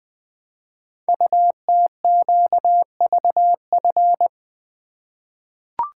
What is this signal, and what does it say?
Morse code, heard as a keyed tone, about 20 wpm, 700 Hz. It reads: UTQVF
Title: Code Groups - 5 Letters - 20wpm